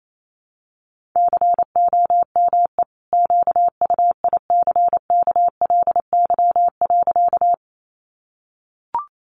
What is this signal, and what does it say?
Morse code, heard as a keyed tone, about 28 wpm, 700 Hz. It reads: COME QUICKLY.
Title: Sentences from Top 700 Words 28wpm